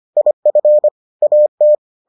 Morse code, heard as a keyed tone, about 25 wpm, 600 Hz. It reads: IF AT